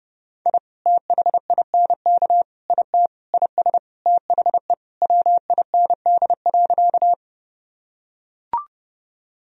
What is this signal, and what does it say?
Morse code, heard as a keyed tone, about 32 wpm, 700 Hz. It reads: I THINK IT IS THE WIND.
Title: Sentences from Top 1000 Words - Review Part 2 - 30wpm